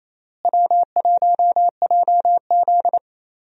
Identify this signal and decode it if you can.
Morse code, heard as a keyed tone, about 28 wpm, 700 Hz. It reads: W1JZ
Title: USA Call Signs - 4 Characters - Repeated 3x - 28wpm